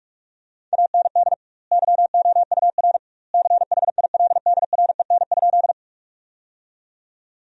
Morse code, heard as a keyed tone, about 42 wpm, 700 Hz. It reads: AND YOUR CHILDREN?
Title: 2x Word-Spacing - Sentences from Top 300 Words - 45wpm